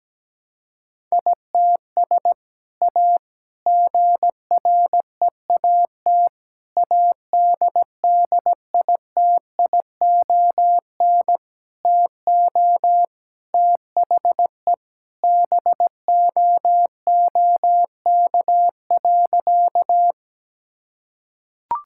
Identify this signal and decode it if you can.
Morse code, heard as a keyed tone, about 17 wpm, 700 Hz. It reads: ITS A GREAT ADDITION TO THE BOOK.
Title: Sentences from Top 1000 Words - Review Part 6 - 17wpm